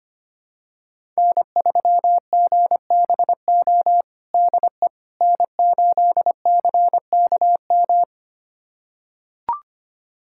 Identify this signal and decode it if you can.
Morse code, heard as a keyed tone, about 25 wpm, 700 Hz. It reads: N3GBO DE N8CKM
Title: USA Callsign DE Callsign - 5 Characters - 25wpm